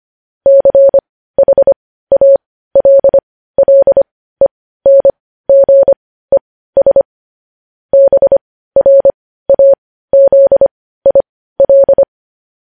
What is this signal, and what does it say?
Morse code, heard as a keyed tone, about 25 wpm, spaced as for 15 wpm, 550 Hz. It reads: CHALLENGES BRAZIL